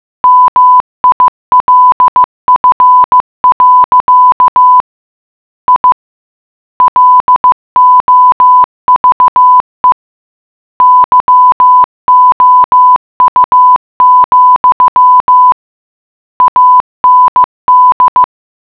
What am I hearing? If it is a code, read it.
Morse code, heard as a keyed tone, about 15 wpm, 1000 Hz. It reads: MILF. I LOVE YOU, AND